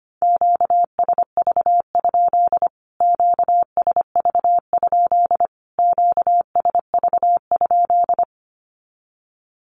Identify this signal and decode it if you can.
Morse code, heard as a keyed tone, about 25 wpm, 700 Hz. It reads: QSV? QSV? QSV?
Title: Q-Codes Repeated 3x 25wpm